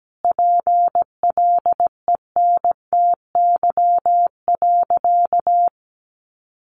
Morse code, heard as a keyed tone, about 17 wpm, 700 Hz. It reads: PLENTY.
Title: Sentences from Top 1300 Words 17wpm